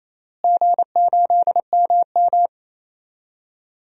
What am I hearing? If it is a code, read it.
Morse code, heard as a keyed tone, about 28 wpm, 700 Hz. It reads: G8MM
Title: DXCC-100 Call Signs - 4 Characters - Repeated 3x - 28wpm